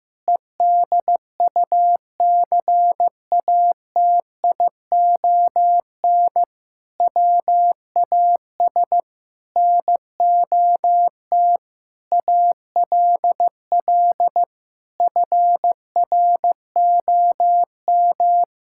Morse code, heard as a keyed tone, about 15 wpm, 700 Hz. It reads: EDUCATION WAS NOT ALL FROM
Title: Sentences from Top 1000 Words - Review Part 6 - 15wpm